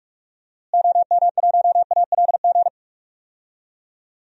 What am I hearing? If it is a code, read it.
Morse code, heard as a keyed tone, about 46 wpm, 700 Hz. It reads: OM1ALG
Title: DXCC-100 Call Signs - 6 Characters - Repeated 2x - 45wpm